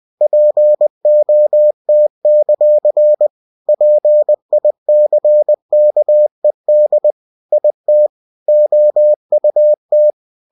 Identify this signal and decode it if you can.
Morse code, heard as a keyed tone, about 20 wpm, 600 Hz. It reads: POT; PICKED IT OUT
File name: cw-books / winnie-the-pooh / winnie-20wpm-page-0058